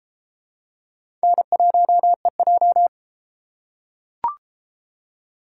Morse code, heard as a keyed tone, about 31 wpm, 700 Hz. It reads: N1EJ